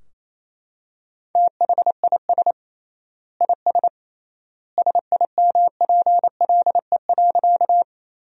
Morse code, heard as a keyed tone, about 27 wpm, 700 Hz. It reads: THIS IS SIMPLE.
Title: 3x Word-Spacing - Sentences from Top 400 Words - 28wpm